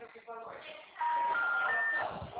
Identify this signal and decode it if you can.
Special information tone, with speech.
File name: out- -2017- - - .19894